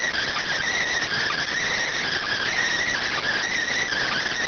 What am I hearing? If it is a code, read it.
Emergency Alert System (SAME) tones, heard as harsh buzzing bursts. It is heard from 0.4 to 4.3 s.